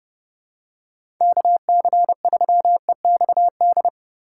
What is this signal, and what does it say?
Morse code, heard as a keyed tone, about 32 wpm, 700 Hz. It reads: KC3EXD
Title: USA Call Signs - 6 Characters - 3x Character Spacing - 30wpm